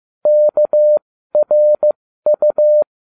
Morse code, heard as a keyed tone, about 15 wpm, spaced as for 13 wpm, 600 Hz. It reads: KRU